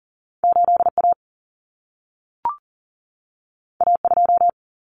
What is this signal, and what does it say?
Morse code, heard as a keyed tone, about 40 wpm, 700 Hz. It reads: 8A A2
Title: Common Cut Numbers - 2-Digit 40wpm